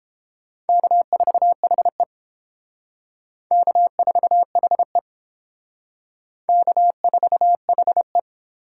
Morse code, heard as a keyed tone, about 31 wpm, 700 Hz. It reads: K4HE K4HE K4HE